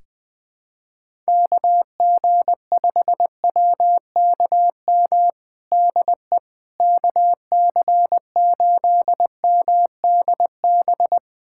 Morse code, heard as a keyed tone, about 20 wpm, 700 Hz. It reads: KG5WKM DE KC8MDB
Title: USA Callsign DE Callsign 20wpm